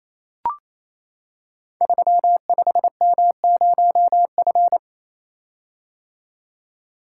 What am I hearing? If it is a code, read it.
Morse code, heard as a keyed tone, about 27 wpm, 700 Hz. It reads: 35M0F
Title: Code Groups - 5 Letter:Numbers - 28wpm